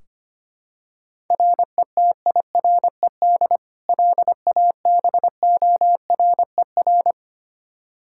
Morse code, heard as a keyed tone, about 25 wpm, 700 Hz. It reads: RETIRED LABORER
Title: Practice QSO - Exchange 3 - More Info - 25wpm